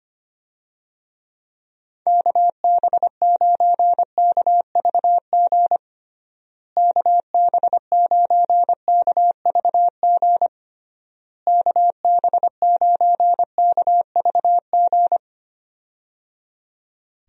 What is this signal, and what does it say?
Morse code, heard as a keyed tone, about 25 wpm, 700 Hz. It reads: KB9KVG KB9KVG KB9KVG
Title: USA Call Signs - 6 Characters - Repeated 3x - 25wpm